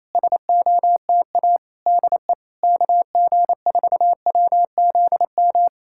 Morse code, heard as a keyed tone, about 28 wpm, 700 Hz. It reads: SOTA DE KG4WZM
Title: SOTA QSO Practice 28wpm